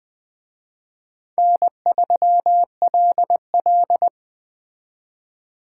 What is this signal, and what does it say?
Morse code, heard as a keyed tone, about 21 wpm, 700 Hz. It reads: N3LL